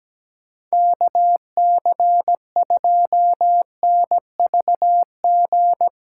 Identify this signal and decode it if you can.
Morse code, heard as a keyed tone, about 17 wpm, 700 Hz. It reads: KC2NVG
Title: USA Call Signs - 6 Characters - 8x Character Spacing - 17wpm